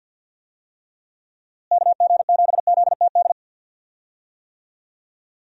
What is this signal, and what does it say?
Morse code, heard as a keyed tone, about 52 wpm, 700 Hz. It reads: KG7ZTD
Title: USA Call Signs - 6 Characters - Repeated 3x - 50wpm